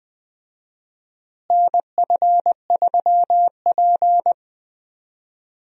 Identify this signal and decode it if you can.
Morse code, heard as a keyed tone, about 20 wpm, 700 Hz. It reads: NF3P
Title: USA Call Signs - 4 Characters - Repeated 2x - 20wpm